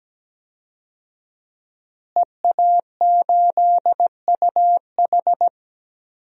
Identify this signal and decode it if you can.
Morse code, heard as a keyed tone, about 17 wpm, 700 Hz. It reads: EA8UH
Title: DXCC-100 Call Signs - 5 Characters - Repeated 2x - 17wpm